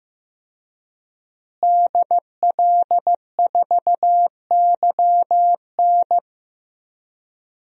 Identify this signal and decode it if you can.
Morse code, heard as a keyed tone, about 15 wpm, 700 Hz. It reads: DL4YN
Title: DXCC-100 Call Signs - 5 Characters - Repeated 2x - 15wpm